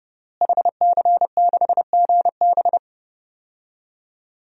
Morse code, heard as a keyed tone, about 30 wpm, 700 Hz. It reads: HC6GB
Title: DXCC-100 Call Signs - 5 Characters - Repeated 2x - 30wpm